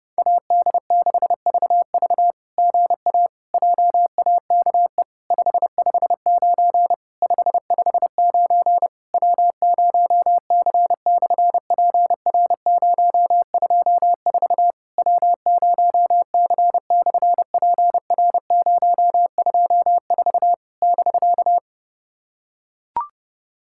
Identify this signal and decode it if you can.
Morse code, heard as a keyed tone, about 30 wpm, 700 Hz. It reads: AD6VV GA JAKE 559 559 W0C/PR024 W0C/PR024 <BK>
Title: SOTA QSO Practice 30wpm